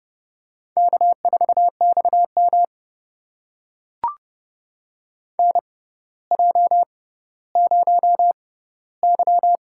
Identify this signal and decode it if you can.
Morse code, heard as a keyed tone, about 30 wpm, spaced as for 15 wpm, 700 Hz. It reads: K4XM NJ0Y
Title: USA Call Signs - 4 Characters - 6x Character Spacing - 30wpm